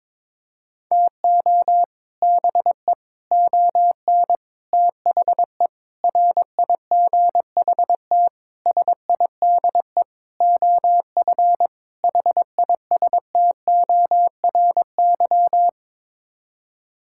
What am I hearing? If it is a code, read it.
Morse code, heard as a keyed tone, about 22 wpm, 700 Hz. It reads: TO BE ON THE RIGHT SIDE OF HISTORY
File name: practice / 22wpm / Sets of 8 Words - 22wpm